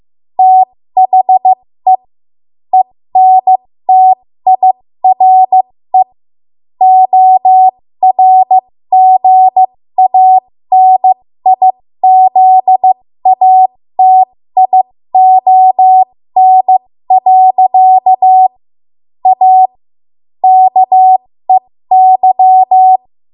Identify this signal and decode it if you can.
Morse code, heard as a keyed tone, about 15 wpm, spaced as for 13 wpm, 750 Hz. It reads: THE ENTIRE ORGANIZATION. A KEY
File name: Morse / 13WPM